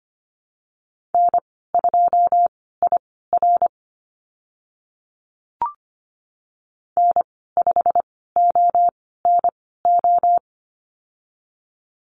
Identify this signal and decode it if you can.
Morse code, heard as a keyed tone, about 25 wpm, spaced as for 16 wpm, 700 Hz. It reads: N2IR N5ONO
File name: Lesson 15g - Number 5 - Call Signs 25wpm